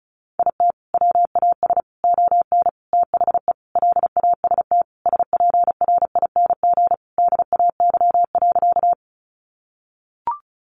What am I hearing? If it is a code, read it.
Morse code, heard as a keyed tone, about 34 wpm, 700 Hz. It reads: IT WAS ON THE LAST SPRING DAY.
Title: Sentences from Top 1200 Words 35wpm